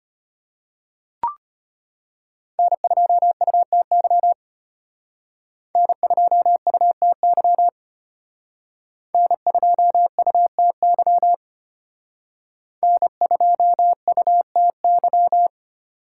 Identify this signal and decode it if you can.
Morse code, heard as a keyed tone, about 31 wpm, 700 Hz. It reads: N2UTY N2UTY N2UTY N2UTY